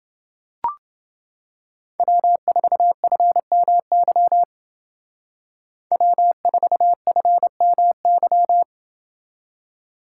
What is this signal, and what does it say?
Morse code, heard as a keyed tone, about 29 wpm, 700 Hz. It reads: W4FMY W4FMY